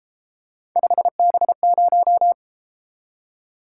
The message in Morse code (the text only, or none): 5B0